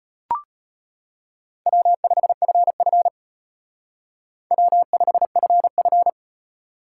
Morse code, heard as a keyed tone, about 30 wpm, 700 Hz. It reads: W5FF W5FF
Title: Speed Racing - USA 1x2 Call Signs 25wpm